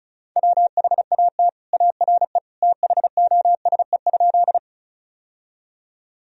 Morse code, heard as a keyed tone, about 34 wpm, 690 Hz. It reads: WHAT ARE THOSE?
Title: Sentences from Top 1000 Words - Review Part 1 - 35wpm